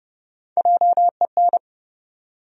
Morse code, heard as a keyed tone, about 30 wpm, 700 Hz. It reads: JEN
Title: USA Common Names - 30wpm